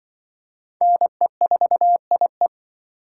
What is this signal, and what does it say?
Morse code, heard as a keyed tone, about 25 wpm, 700 Hz. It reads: NE4IE
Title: Speed Racing - USA Call Signs 20wpm